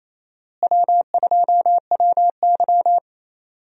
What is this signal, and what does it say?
Morse code, heard as a keyed tone, about 28 wpm, 700 Hz. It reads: W2WY
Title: USA Call Signs - 4 Characters - Repeated 3x - 28wpm